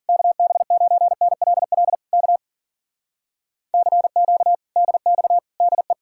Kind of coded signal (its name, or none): Morse code